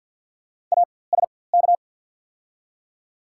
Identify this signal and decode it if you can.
Morse code, heard as a keyed tone, about 48 wpm, spaced as for 23 wpm, 690 Hz. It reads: ASK
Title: 4x Character Spacing - Top 500 Words - 50wpm